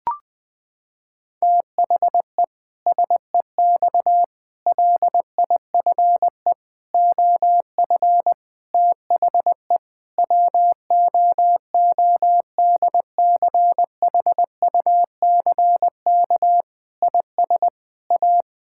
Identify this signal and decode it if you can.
Morse code, heard as a keyed tone, about 20 wpm, 700 Hz. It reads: THE SEX LIFE OF THE WOODCHUCK IS A
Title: Pangram Challenge v2 - 20wpm